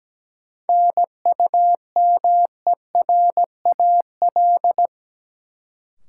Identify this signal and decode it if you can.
Morse code, heard as a keyed tone, about 17 wpm, 700 Hz. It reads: NUMERAL
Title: Sentences from Top 1000 Words - Review Part 2 - New Word Drill - 17wpm